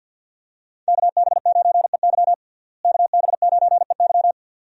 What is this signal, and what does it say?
Morse code, heard as a keyed tone, about 50 wpm, 700 Hz. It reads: KB9EY KB9EY
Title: USA Call Signs - 5 Characters - Repeated 2x - 50wpm